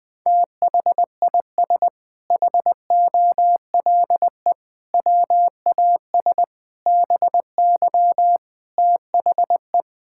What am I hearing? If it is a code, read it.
Morse code, heard as a keyed tone, about 21 wpm, 700 Hz. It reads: THIS HOLE WAS BY THE
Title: Sentences from Top 1600 Words 20wpm